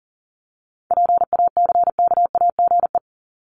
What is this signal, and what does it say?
Morse code, heard as a keyed tone, about 40 wpm, 700 Hz. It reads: PACKAGE
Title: Top 1501-1750 Words 40wpm